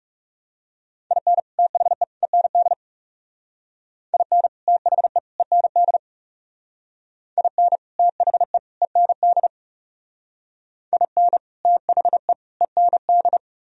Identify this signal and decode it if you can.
Morse code, heard as a keyed tone, about 40 wpm, 700 Hz. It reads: IN THE END IN THE END IN THE END IN THE END